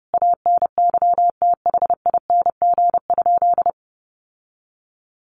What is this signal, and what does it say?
Morse code, heard as a keyed tone, about 30 wpm, 700 Hz. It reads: ANYTHING?
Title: Sentences from Top 1000 Words 30wpm